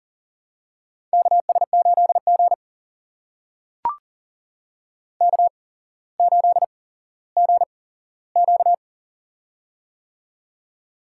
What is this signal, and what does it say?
Morse code, heard as a keyed tone, about 40 wpm, spaced as for 17 wpm, 700 Hz. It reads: KS8G K8GQ
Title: USA Call Signs - 4 Characters - 8x Character Spacing - 40wpm